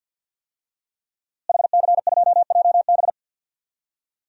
Morse code, heard as a keyed tone, about 52 wpm, 700 Hz. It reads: HC2JB
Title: DXCC-100 Call Signs - 5 Characters - Repeated 2x - 50wpm